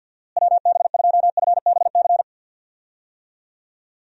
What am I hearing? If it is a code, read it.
Morse code, heard as a keyed tone, about 51 wpm, 700 Hz. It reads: WB2FBC
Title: USA Call Signs - 6 Characters - Repeated 3x - 50wpm